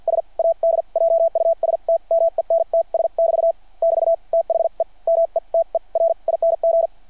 Morse code, heard as a keyed tone, about 52 wpm, spaced as for 41 wpm, 660 Hz. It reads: S ADJUSTMENTS <BT> <BT> THE METERING
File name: archived / CRSnet